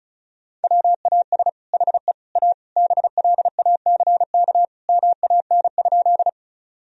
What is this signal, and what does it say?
Morse code, heard as a keyed tone, about 36 wpm, 700 Hz. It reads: WAS HE A BLACK MAN?
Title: Sentences from Top 1000 Words - Review Part 2 - 35wpm